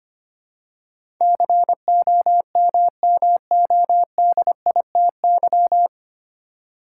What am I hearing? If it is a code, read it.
Morse code, heard as a keyed tone, about 25 wpm, 700 Hz. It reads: COMMODITY
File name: practice / 25wpm / 9 Letter Words - First 3 Letters Given - 25wpm